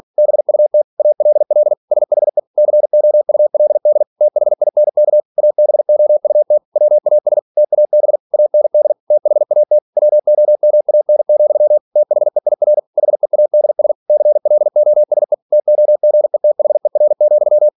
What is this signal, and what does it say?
Morse code, heard as a keyed tone, about 46 wpm, 600 Hz. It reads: BUT ALL SHE COULD THINK ABOUT WAS TAB AND THAT WOMAN, THEIR HEADS CLOSE TOGETHER,
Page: Never 47wpm